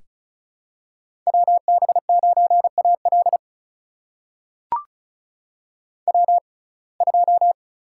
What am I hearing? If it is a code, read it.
Morse code, heard as a keyed tone, about 36 wpm, spaced as for 25 wpm, 700 Hz. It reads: WB9AL W2
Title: USA Call Signs - 6x Character Spacing - 35wpm